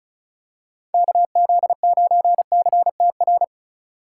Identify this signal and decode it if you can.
Morse code, heard as a keyed tone, about 36 wpm, 700 Hz. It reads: KZ9CTR
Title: USA Call Signs - 6 Characters - 4x Character Spacing - 35wpm